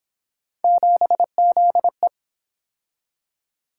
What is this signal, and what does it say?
Morse code, heard as a keyed tone, about 28 wpm, 700 Hz. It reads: 7ZE